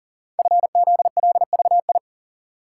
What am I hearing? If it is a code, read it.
Morse code, heard as a keyed tone, about 43 wpm, 700 Hz. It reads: FZLVI
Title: Code Groups - 5 Letters - 40wpm